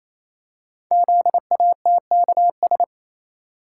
Morse code, heard as a keyed tone, about 27 wpm, 700 Hz. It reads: ZATKS
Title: Code Groups - 5 Letters - 28wpm